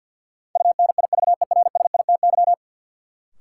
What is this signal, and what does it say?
Morse code, heard as a keyed tone, about 53 wpm, 700 Hz. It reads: UNIVERSITY
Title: Sentences from Top 1300 Words - New Word Drill Alt - 50wpm